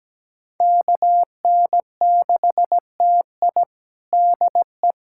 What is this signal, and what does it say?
Morse code, heard as a keyed tone, about 17 wpm, 700 Hz. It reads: KN6TI DE
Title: USA Callsign DE Callsign - 5 Characters - 17wpm